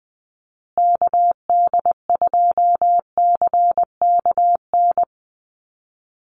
Morse code, heard as a keyed tone, about 20 wpm, 700 Hz. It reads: KD2CKN
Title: USA Call Signs Repeated 2x 20wpm